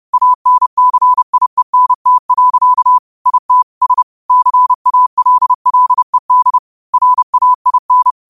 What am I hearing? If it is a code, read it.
Morse code, heard as a keyed tone, about 30 wpm, 1000 Hz. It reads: ANCIENT. IT S CALLED RAIN
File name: qotdfiles / 30wpm